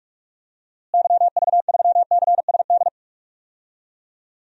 Morse code, heard as a keyed tone, about 46 wpm, 700 Hz. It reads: YV3CSD
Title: DXCC-100 Call Signs - 6 Characters - Repeated 2x - 45wpm